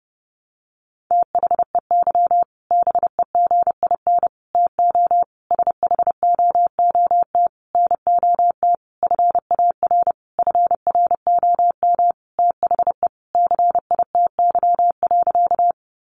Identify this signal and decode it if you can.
Morse code, heard as a keyed tone, about 30 wpm, 700 Hz. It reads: THEY BEGIN TO SHOOT NOT FAR FROM THE CITY.
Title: Sentences from Top 1200 Words 30wpm